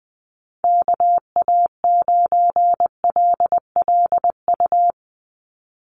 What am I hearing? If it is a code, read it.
Morse code, heard as a keyed tone, about 20 wpm, 700 Hz. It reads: KA9LLU
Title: USA Call Signs 20wpm